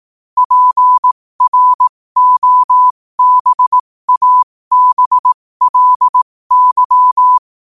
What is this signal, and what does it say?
Morse code, heard as a keyed tone, about 18 wpm, spaced as for 15 wpm, 1000 Hz. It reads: PROBABLY
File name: qotdfiles / 15wpm